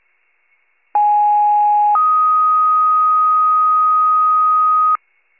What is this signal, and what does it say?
Two-tone page, heard as a one-second tone, then a three-second tone, about 830 then 1250 Hz. It is heard from 0.9 to 5.0 s.